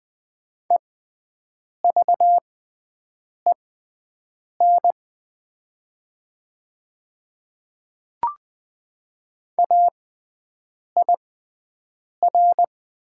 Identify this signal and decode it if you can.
Morse code, heard as a keyed tone, about 20 wpm, spaced as for 7 wpm, 700 Hz. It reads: EVEN AIR
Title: 6x Character Spacing - Top 200 Words - 20wpm